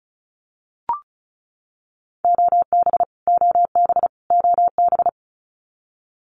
Morse code, heard as a keyed tone, about 34 wpm, 700 Hz. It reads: OB OB OB
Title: Abbreviations Repeated 3x - 35wpm